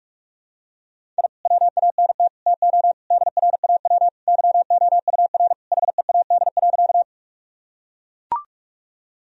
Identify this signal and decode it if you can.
Morse code, heard as a keyed tone, about 44 wpm, 700 Hz. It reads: I WANT TO DRAW YOUR HEAD.
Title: Sentences from Top 1000 Words - Review Part 1 - 45wpm